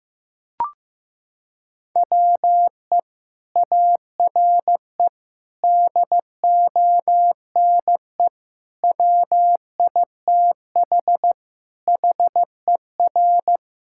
Morse code, heard as a keyed tone, about 16 wpm, 700 Hz. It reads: WE ARE DONE WITH HER
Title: Sentences from Top 1600 Words 15wpm